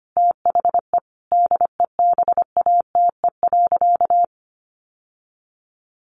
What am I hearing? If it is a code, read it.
Morse code, heard as a keyed tone, about 25 wpm, 700 Hz. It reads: THE DEBATE.
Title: Sentences from Top 1100 Words 25wpm